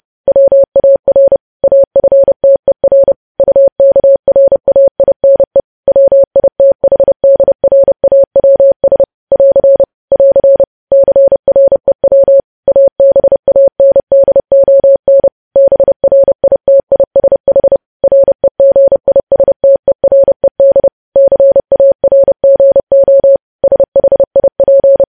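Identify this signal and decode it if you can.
Morse code, heard as a keyed tone, about 30 wpm, 550 Hz. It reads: WAR AFTER UKRAINE WITHDRAWS <AR> <AR> CREW ABANDON BRITISH REGISTERED CARGO SHIP
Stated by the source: Morse code news headlines sent at 30 WPM.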